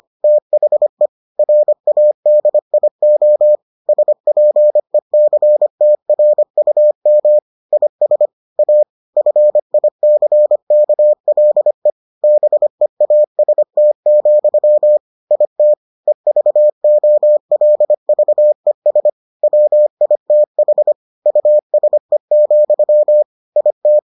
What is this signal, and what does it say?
Morse code, heard as a keyed tone, about 25 wpm, 600 Hz. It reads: THE RADIO SPECTRUM IS A FICKLE BEAST, IT EVOLVES WITH USE, IT